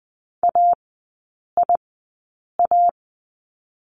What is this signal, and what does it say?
Morse code, heard as a keyed tone, about 20 wpm, spaced as for 8 wpm, 700 Hz. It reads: AIA
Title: 041 Lesson 8e - Number 1 - Three Characters 20wpm